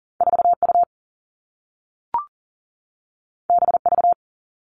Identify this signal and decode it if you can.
Morse code, heard as a keyed tone, about 42 wpm, 700 Hz. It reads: 4 U B V
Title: Two Challenging Letter-Number Combinations 40wpm